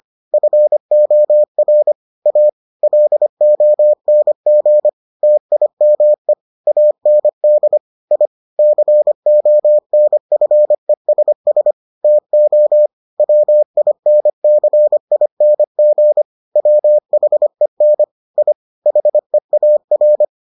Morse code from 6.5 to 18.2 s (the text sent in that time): AND I CONFESS TO WINCING WHEN